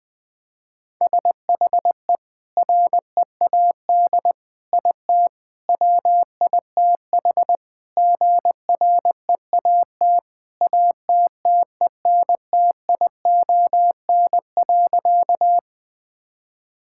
Morse code, heard as a keyed tone, about 20 wpm, 700 Hz. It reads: SHE READ IT WITH GREAT ATTENTION.
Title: Sentences from Top 1000 Words - Review Part 6 - 20wpm